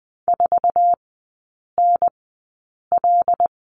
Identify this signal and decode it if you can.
Morse code, heard as a keyed tone, about 20 wpm, spaced as for 8 wpm, 700 Hz. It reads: 4NL
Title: 074 Lesson 13e - Letter L - Three Characters 20wpm